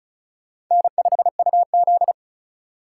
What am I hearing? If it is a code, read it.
Morse code, heard as a keyed tone, about 45 wpm, 700 Hz. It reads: N5UZ